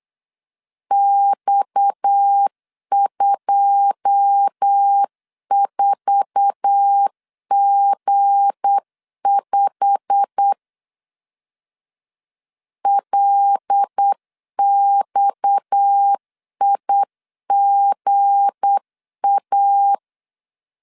Morse code, heard as a keyed tone, about 8 wpm, 800 Hz. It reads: X24G5 LXIGA
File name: CW / DET / 1 9 1500C S A 8wpm 5637